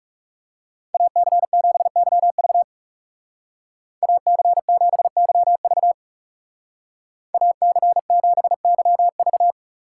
Morse code, heard as a keyed tone, about 35 wpm, 700 Hz. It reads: AC7YV AC7YV AC7YV
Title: Speed Racing - USA 2x2 Call Signs 30wpm